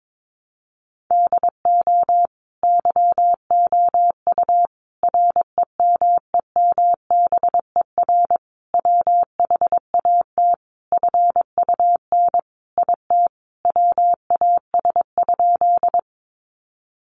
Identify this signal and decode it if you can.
Morse code, heard as a keyed tone, about 22 wpm, 700 Hz. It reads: DO YOU REMEMBER WHAT FUN IT WAS?
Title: Sentences from Top 700 Words 22wpm